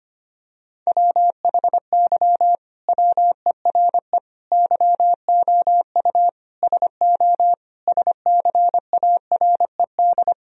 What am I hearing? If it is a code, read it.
Morse code, heard as a keyed tone, about 25 wpm, 700 Hz. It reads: WHY WERE YOU SO SCARED
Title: Sentences from Top 1000 Words - Review Part 4 - 25wpm